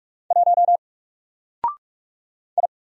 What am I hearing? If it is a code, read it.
Morse code, heard as a keyed tone, about 46 wpm, 700 Hz. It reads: 1 I
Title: Single Letter-Number 45wpm